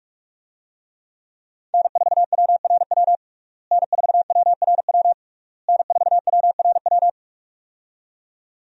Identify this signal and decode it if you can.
Morse code, heard as a keyed tone, about 41 wpm, 700 Hz. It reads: N4WRW N4WRW N4WRW
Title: USA Call Signs - 5 Characters - Repeated 3x - 45wpm